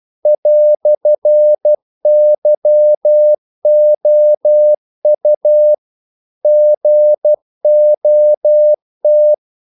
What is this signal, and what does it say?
Morse code, heard as a keyed tone, about 12 wpm, 600 Hz. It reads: "YOU GOT
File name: CW / Practice / EnglishQuotations0312